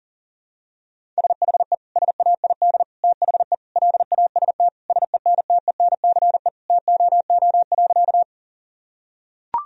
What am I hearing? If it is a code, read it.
Morse code, heard as a keyed tone, about 40 wpm, 700 Hz. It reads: SHE SAID THE LAST SENTENCE TOO.
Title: Sentences from Top 1000 Words - Review Part 1 - 40wpm